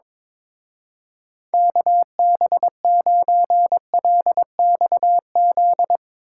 Morse code, heard as a keyed tone, about 24 wpm, 700 Hz. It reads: KB9LXZ
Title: Speed Racing - USA Call Signs 22wpm